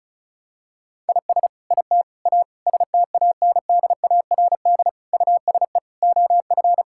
Morse code, heard as a keyed tone, about 36 wpm, 700 Hz. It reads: IS IT A STANDARD USE OF
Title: Sentences from Top 1000 Words - Review Part 6 - 35wpm